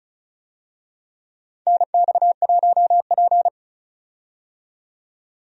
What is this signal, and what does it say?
Morse code, heard as a keyed tone, about 36 wpm, 690 Hz. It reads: NX1P
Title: USA Call Signs - 4 Characters - 35wpm